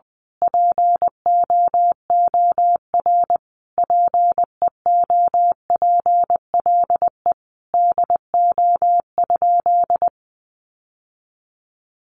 Morse code, heard as a keyed tone, about 20 wpm, 700 Hz. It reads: POOR PEOPLE DO?
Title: Sentences from Top 1100 Words 20wpm